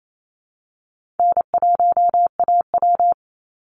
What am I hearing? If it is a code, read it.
Morse code, heard as a keyed tone, about 28 wpm, 700 Hz. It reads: N1AW
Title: NA SOTA Chaser Callsigns 28wpm